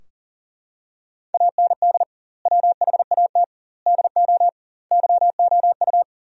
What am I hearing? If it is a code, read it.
Morse code, heard as a keyed tone, about 40 wpm, 700 Hz. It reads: AND WHAT DO YOU
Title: 2x Word-Spacing - Sentences from Top 100 Words - 40wpm